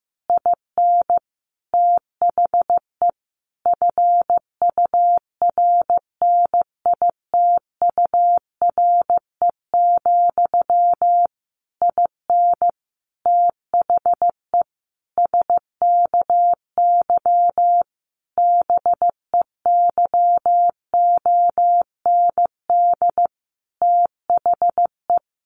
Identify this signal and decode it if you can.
Morse code, heard as a keyed tone, about 15 wpm, 700 Hz. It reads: IN THE FURNITURE, IN THE SKY BEYOND THE
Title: A Sound Of Thunder 15wpm